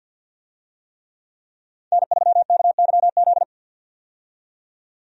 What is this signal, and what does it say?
Morse code, heard as a keyed tone, about 51 wpm, 700 Hz. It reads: N3KYZ
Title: USA Call Signs - 5 Characters - 50wpm